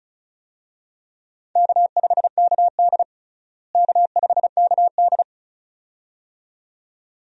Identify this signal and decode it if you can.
Morse code, heard as a keyed tone, about 36 wpm, 700 Hz. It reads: K5KD K5KD
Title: USA Call Signs - 4 Characters - Repeated 2x - 35wpm